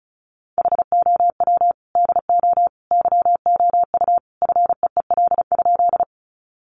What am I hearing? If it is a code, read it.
Morse code, heard as a keyed tone, about 34 wpm, 700 Hz. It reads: HOW DO YOU FEEL?
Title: Sentences from Top 400 Words 35wpm